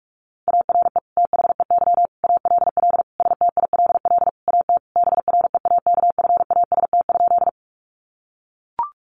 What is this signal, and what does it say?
Morse code, heard as a keyed tone, about 44 wpm, 700 Hz. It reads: ARE THEY ALL STILL AT BREAKFAST?
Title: Sentences from Top 1500 Words 45wpm